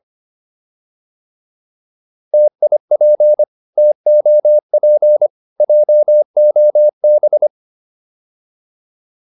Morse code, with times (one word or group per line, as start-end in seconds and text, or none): 2.3-3.4 TIP
3.8-5.3 TOP
5.6-7.5 JOB